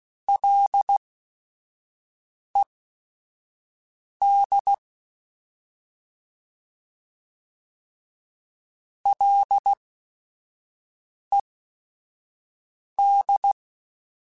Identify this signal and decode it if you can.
Morse code, heard as a keyed tone, about 16 wpm, spaced as for 5 wpm, 800 Hz. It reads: LED LED
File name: Ls / Single L m3s / Single L @5